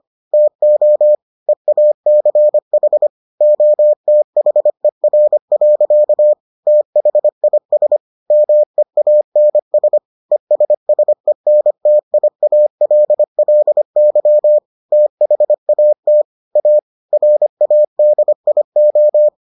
Morse code, heard as a keyed tone, about 25 wpm, 600 Hz. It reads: TO EACH OTHER. THIS MEANS ESSENTIALLY THAT A RADIO